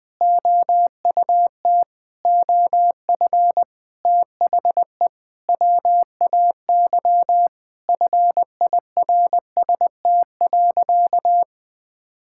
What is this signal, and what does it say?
Morse code, heard as a keyed tone, about 20 wpm, 700 Hz. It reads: OUT OF THE WAY FIRST.
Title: Sentences with 7-Word Phrases - 20wpm